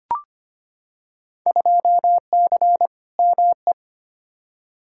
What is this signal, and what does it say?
Morse code, heard as a keyed tone, about 25 wpm, 700 Hz. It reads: 2C ME
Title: Field Day Exchanges Encore - 25wpm